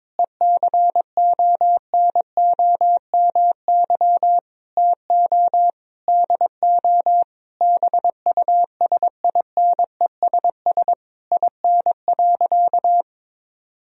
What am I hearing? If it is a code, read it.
Morse code, heard as a keyed tone, about 22 wpm, 700 Hz. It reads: ECONOMY TO DO BUSINESS IN.
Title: Sentences from Top 1000 Words - Review Part 3 - 22wpm